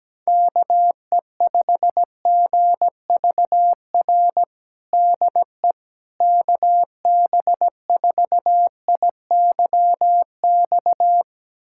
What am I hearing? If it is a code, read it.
Morse code, heard as a keyed tone, about 17 wpm, 700 Hz. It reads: KE5GVR DE KB4IYX
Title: USA Callsign DE Callsign 17wpm